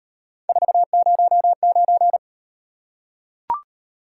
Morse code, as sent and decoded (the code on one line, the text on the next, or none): ....- ----- ----.
409